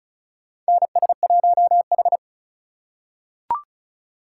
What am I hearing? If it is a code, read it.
Morse code, heard as a keyed tone, about 34 wpm, 700 Hz. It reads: NS1H
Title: USA Call Signs - 4 Characters - 4x Character Spacing - 35wpm